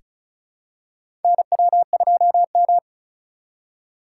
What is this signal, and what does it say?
Morse code, heard as a keyed tone, about 34 wpm, 700 Hz. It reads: NW2M
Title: USA Call Signs - 6x Character Spacing - 35wpm